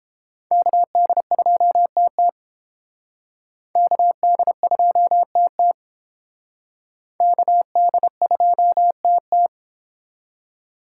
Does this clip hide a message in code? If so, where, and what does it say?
Yes: Morse code from 0.5 to 9.5 s, heard as a keyed tone, about 28 wpm, 700 Hz. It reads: KD2TT KD2TT KD2TT